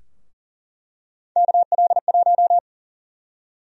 Morse code, heard as a keyed tone, about 37 wpm, 700 Hz. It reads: KL1
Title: USA Call Sign Prefixes - Repeated 2x - 40wpm